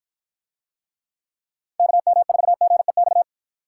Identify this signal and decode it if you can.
Morse code, heard as a keyed tone, about 53 wpm, 700 Hz. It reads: KM4GEX